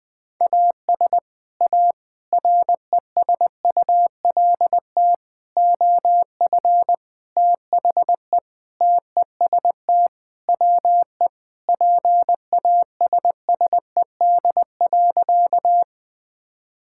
Morse code, heard as a keyed tone, about 20 wpm, 700 Hz. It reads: AS A RESULT OF THE TEST WE PASSED.
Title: Sentences with 4-Word Phrases - 20wpm